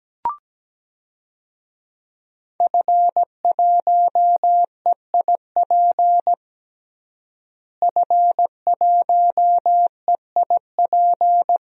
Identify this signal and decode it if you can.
Morse code, heard as a keyed tone, about 17 wpm, 700 Hz. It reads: F1EIP F1EIP
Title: DXCC-100 Call Signs - 5 Characters - Repeated 3x - 17wpm